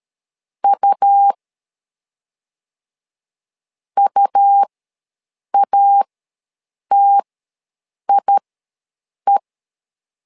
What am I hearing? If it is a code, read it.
Morse code, heard as a keyed tone, about 13 wpm, spaced as for 7 wpm, 800 Hz. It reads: U UATIE
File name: LEZIONI / 2 UHDN / 2 U ws 12wpm 1655